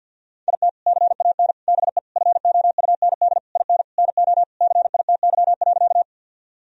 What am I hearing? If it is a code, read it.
Morse code, heard as a keyed tone, about 49 wpm, 700 Hz. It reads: IT CAN BE FOUND IN NO CITY.
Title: Sentences from Top 1000 Words - Review Part 1 - 50wpm